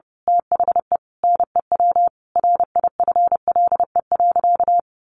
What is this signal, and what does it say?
Morse code, heard as a keyed tone, about 28 wpm, 700 Hz. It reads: THE NEW RIFLE.
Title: A Sound Of Thunder 30wpm